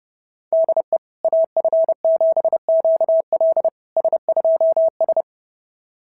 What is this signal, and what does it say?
Morse code, heard as a keyed tone, about 30 wpm, 690 Hz. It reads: DE AF7QL S2S
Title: SOTA QSO Practice 30wpm